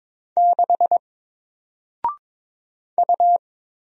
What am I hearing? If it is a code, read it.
Morse code, heard as a keyed tone, about 22 wpm, 700 Hz. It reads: 6 U
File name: Single-Letter-Number-22wpm